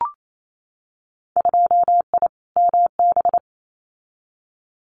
Morse code, heard as a keyed tone, about 28 wpm, 700 Hz. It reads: 2I MB
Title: Winter Field Day Exchanges Encore 28wpm